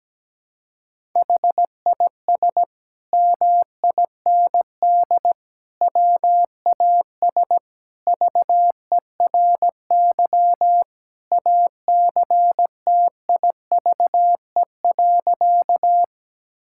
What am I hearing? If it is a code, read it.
Morse code, heard as a keyed tone, about 17 wpm, 700 Hz. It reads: HIS MIND WAS VERY ACTIVE.
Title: Sentences from Top 1000 Words - Review Part 6 - 17wpm